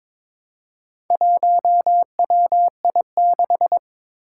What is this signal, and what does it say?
Morse code, heard as a keyed tone, about 23 wpm, 700 Hz. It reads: 1WI6
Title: Four Challenging Letter-Number Combinations 22wpm